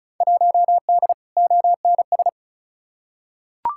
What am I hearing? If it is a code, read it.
Morse code, heard as a keyed tone, about 36 wpm, 700 Hz. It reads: 1D ONS
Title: Field Day Exchanges - 35wpm